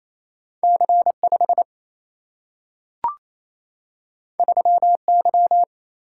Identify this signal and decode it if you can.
Morse code, heard as a keyed tone, about 28 wpm, 700 Hz. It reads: C5 3Y
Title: Single-Letter Single-Number 28wpm